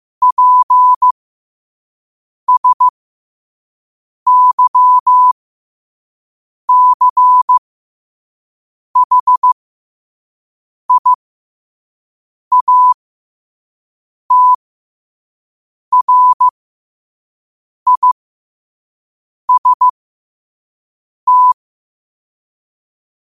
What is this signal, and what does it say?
Morse code, heard as a keyed tone, about 15 wpm, spaced as for 5 wpm, 1000 Hz. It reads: PSYCHIATRIST
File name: qotdfiles / 5wpm